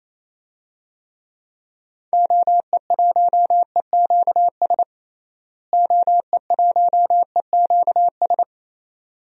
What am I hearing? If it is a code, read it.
Morse code, heard as a keyed tone, about 28 wpm, 700 Hz. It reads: OE1EQS OE1EQS
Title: DXCC-100 Call Signs - 6 Characters - Repeated 3x - 28wpm